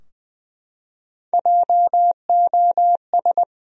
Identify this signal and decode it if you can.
Morse code, heard as a keyed tone, about 20 wpm, 700 Hz. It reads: JOS
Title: USA Common Names - Max 5 Letters - 20wpm